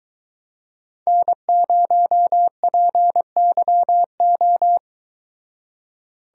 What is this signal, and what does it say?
Morse code, heard as a keyed tone, about 22 wpm, 700 Hz. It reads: N0PYO